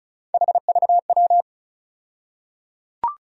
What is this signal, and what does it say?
Morse code, heard as a keyed tone, about 36 wpm, 700 Hz. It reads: HVW
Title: USA Call Sign Postfixes - 3 Letters 35wpm